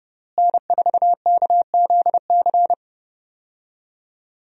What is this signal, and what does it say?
Morse code, heard as a keyed tone, about 30 wpm, 700 Hz. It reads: N4KZC
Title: USA Call Signs - 5 Characters - Repeated 3x - 30wpm